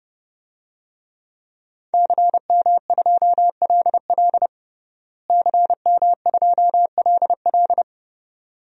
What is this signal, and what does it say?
Morse code, heard as a keyed tone, about 30 wpm, 700 Hz. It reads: CM2LL CM2LL
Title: DXCC-100 Call Signs - 5 Characters - Repeated 2x - 30wpm